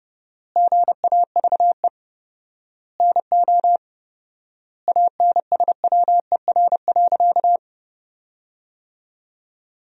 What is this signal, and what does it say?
Morse code, heard as a keyed tone, about 32 wpm, 700 Hz. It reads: GAVE NO ANSWER.
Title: 4x Word-Spacing - Sentences from Top 500 Words - 30wpm